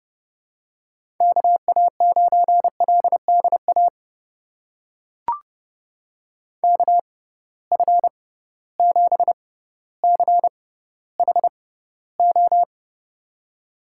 Morse code, heard as a keyed tone, about 31 wpm, spaced as for 16 wpm, 700 Hz. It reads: KA9LDA KF7CHO
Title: USA Call Signs - 6 Characters - 6x Character Spacing - 30wpm